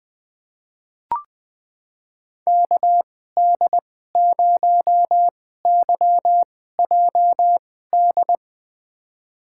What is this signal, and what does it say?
Morse code, heard as a keyed tone, about 20 wpm, spaced as for 14 wpm, 700 Hz. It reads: KD0YJD
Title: USA Call Signs - 2x Character Spacing - 20wpm